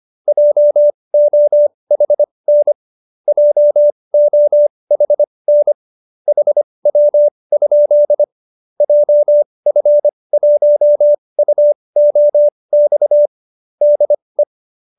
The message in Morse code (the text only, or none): JOHN JOHN HW? JF1UOX DE